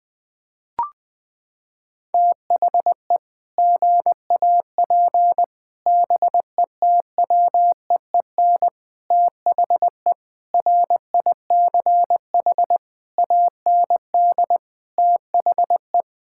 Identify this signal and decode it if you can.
Morse code, heard as a keyed tone, about 20 wpm, 700 Hz. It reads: THE GAP BETWEEN THE RICH AND THE
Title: Sentences with 8-Word Phrases - 20wpm